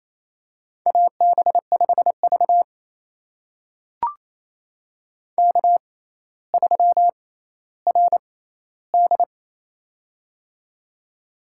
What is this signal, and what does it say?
Morse code, heard as a keyed tone, about 28 wpm, spaced as for 14 wpm, 700 Hz. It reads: AB5V K3RD
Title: USA Call Signs - 4 Characters - 6x Character Spacing - 28wpm